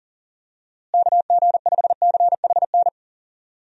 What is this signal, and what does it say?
Morse code, heard as a keyed tone, about 39 wpm, 700 Hz. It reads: KG5CHN
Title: USA Call Signs - 6 Characters - Repeated 3x - 40wpm